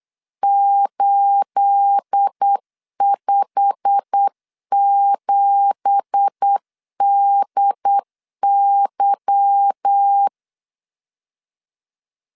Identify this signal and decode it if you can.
Morse code, heard as a keyed tone, about 8 wpm, 800 Hz. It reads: 857DY
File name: LEZIONI / 10 SIGNS / 10B INT 8wpm 3534